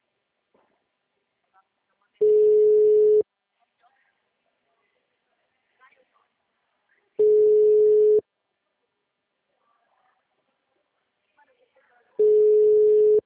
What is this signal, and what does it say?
Ringback tone (Europe, 425 Hz), with speech.
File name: out- -2004- - - .2